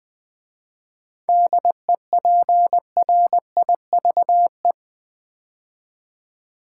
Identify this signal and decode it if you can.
Morse code, heard as a keyed tone, about 20 wpm, 700 Hz. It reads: DEPRIVE
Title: 7 Letter Words - First 3 Letters Given - 20wpm